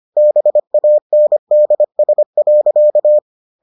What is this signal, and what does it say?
Morse code, heard as a keyed tone, about 25 wpm, 600 Hz. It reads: BANDS.